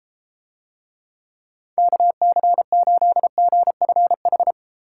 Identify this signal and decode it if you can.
Morse code, heard as a keyed tone, about 32 wpm, 700 Hz. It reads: KC8GFH